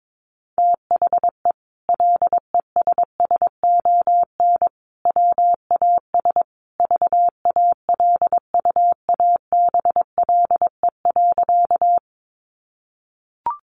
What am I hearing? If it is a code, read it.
Morse code, heard as a keyed tone, about 22 wpm, 700 Hz. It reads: THE LESSON WAS VALUABLE.
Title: Sentences from Top 1200 Words 22wpm